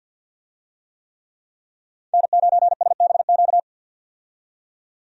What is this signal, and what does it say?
Morse code, heard as a keyed tone, about 51 wpm, 700 Hz. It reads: N9SBQ
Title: USA Call Signs - 5 Characters - 50wpm